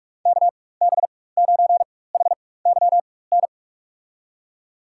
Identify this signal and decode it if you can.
Morse code, heard as a keyed tone, about 46 wpm, spaced as for 21 wpm, 700 Hz. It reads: KB9HYN
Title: USA Call Signs - 6 Characters - 4x Character Spacing - 45wpm